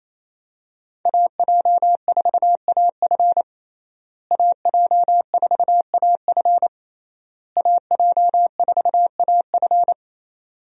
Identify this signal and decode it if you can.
Morse code, heard as a keyed tone, about 28 wpm, 700 Hz. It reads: AJ4AF AJ4AF AJ4AF
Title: USA Call Signs - 5 Characters - Repeated 3x - 28wpm